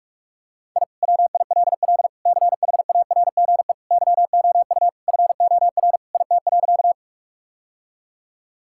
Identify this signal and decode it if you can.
Morse code, heard as a keyed tone, about 46 wpm, 700 Hz. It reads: I WILL CHARGE YOU FOR IT.
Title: Sentences from Top 1000 Words - Review Part 6 - 45wpm